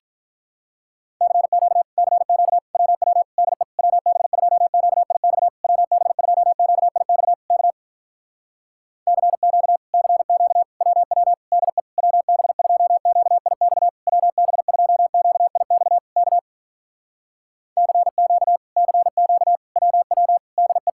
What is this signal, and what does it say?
Morse code, heard as a keyed tone, about 46 wpm, 700 Hz. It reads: CQ CQ WW DE WB2QIX WB2QIX K CQ CQ WW DE WB2QIX WB2QIX K CQ CQ WW DE